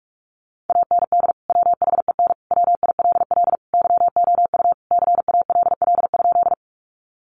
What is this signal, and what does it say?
Morse code, heard as a keyed tone, about 44 wpm, 700 Hz. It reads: AND WHEN WILL YOU CALL?
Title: Sentences from Top 100 Words 45wpm